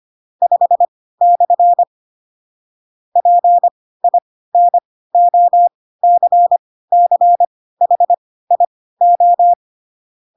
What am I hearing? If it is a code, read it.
Morse code, heard as a keyed tone, about 25 wpm, spaced as for 16 wpm, 700 Hz. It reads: 5/ PINOCCHIO
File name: Pinocchio / Pinocchio ch5 10 25 ik1hge